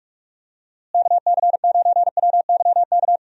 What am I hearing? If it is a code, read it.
Morse code, heard as a keyed tone, about 41 wpm, 700 Hz. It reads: KC9WYK
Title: USA Call Signs - 6 Characters - Repeated 2x - 45wpm